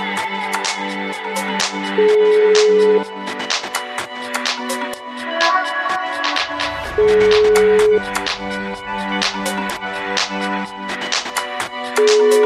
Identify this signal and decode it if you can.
Ringback tone (Europe, 425 Hz), with noise.